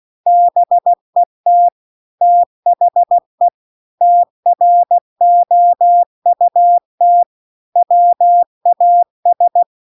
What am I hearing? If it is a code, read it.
Morse code, heard as a keyed tone, about 16 wpm, 700 Hz. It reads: BET THE TROUT WAS
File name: CW-The-Big-Twoheaded-River-2-16WPM